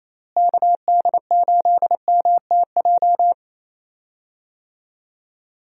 Morse code, heard as a keyed tone, about 28 wpm, 700 Hz. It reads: KD8MTJ
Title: USA Call Signs - 6 Characters - Repeated 3x - 28wpm